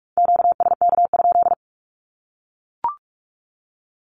Morse code, heard as a keyed tone, about 44 wpm, 700 Hz. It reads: QSK?
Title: Q-Codes 45wpm